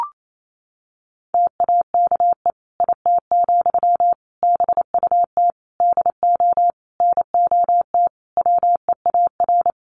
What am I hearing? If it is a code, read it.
Morse code, heard as a keyed tone, about 28 wpm, 700 Hz. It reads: TAKE IT, BUT DO NOT WEAR
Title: Sentences from Top 1500 Words 28wpm